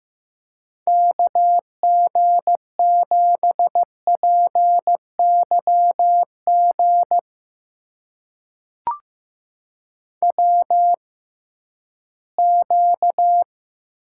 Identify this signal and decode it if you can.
Morse code, heard as a keyed tone, about 15 wpm, spaced as for 11 wpm, 700 Hz. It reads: KG7PYG WQ
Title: USA Call Signs - 6x Character Spacing - 15wpm